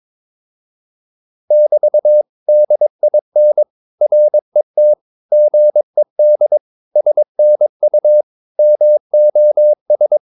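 Morse code, heard as a keyed tone, about 22 wpm, 600 Hz. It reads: <BT> DIN RET GED SNU MOS
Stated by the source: Morse practice of short Danish words at 22 wpm, from CW med Gnister.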